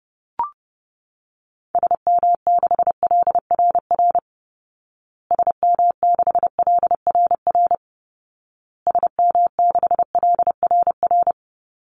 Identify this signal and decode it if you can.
Morse code, heard as a keyed tone, about 30 wpm, 700 Hz. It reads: SM6LRR SM6LRR SM6LRR
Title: Call Signs Repeated 2x - 30wpm